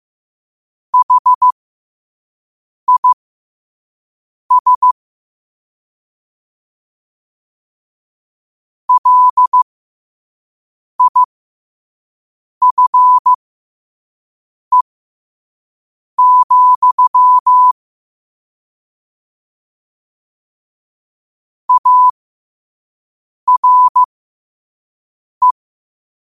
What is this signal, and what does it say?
Morse code, heard as a keyed tone, about 15 wpm, spaced as for 5 wpm, 1000 Hz. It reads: HIS LIFE, ARE
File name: qotdfiles / 5wpm